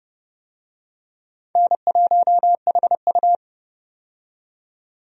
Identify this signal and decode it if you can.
Morse code, heard as a keyed tone, about 30 wpm, 700 Hz. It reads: N1HU
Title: USA Call Signs - 4 Characters - 30wpm